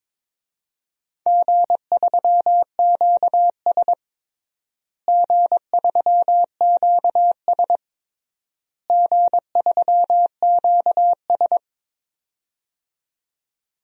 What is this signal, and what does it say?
Morse code, heard as a keyed tone, about 22 wpm, 700 Hz. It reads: G3QS G3QS G3QS
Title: DXCC-100 Call Signs - 4 Characters - Repeated 3x - 22wpm